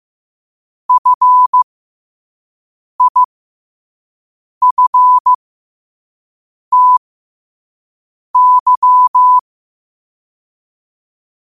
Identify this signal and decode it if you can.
Morse code, heard as a keyed tone, about 15 wpm, spaced as for 5 wpm, 1000 Hz. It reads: FIFTY
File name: qotdfiles / 5wpm